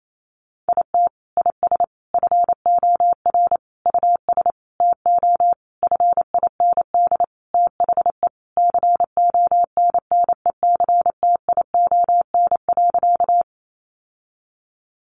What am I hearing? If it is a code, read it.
Morse code, heard as a keyed tone, about 28 wpm, 700 Hz. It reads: IT IS FOR US TO FIND THE CONNECTION.
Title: Sentences from Top 1300 Words 28wpm